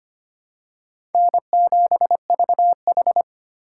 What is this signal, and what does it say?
Morse code, heard as a keyed tone, about 25 wpm, 700 Hz. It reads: N7VH
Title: USA Call Signs - 4 Characters - 2x Character Spacing - 25wpm